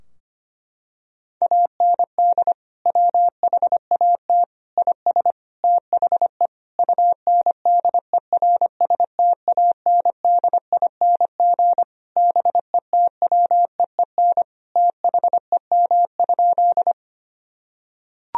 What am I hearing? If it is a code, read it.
Morse code, heard as a keyed tone, about 25 wpm, 700 Hz. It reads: AND WHAT IS THE UNDERSTANDING BETWEEN THEM?
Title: Sentences from Top 1000 Words - Review Part 3 - 25wpm